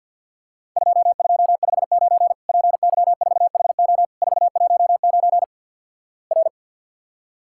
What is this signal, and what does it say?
Morse code, heard as a keyed tone, about 52 wpm, 700 Hz. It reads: 2259 PY4HO 419 R
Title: Worked All Europe DX Contest - QTC Exchange - 50wpm